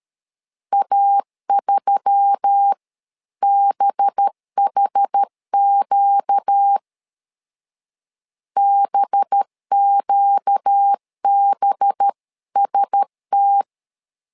Morse code, heard as a keyed tone, about 13 wpm, 800 Hz. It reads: A3BHQ BQBST
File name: LEZIONI / 5 LBVC / 5 LB 12wpm 1906